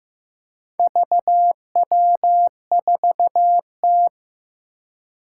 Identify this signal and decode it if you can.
Morse code, heard as a keyed tone, about 15 wpm, 700 Hz. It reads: VW4T
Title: Four Challenging Letter-Number Combinations 15wpm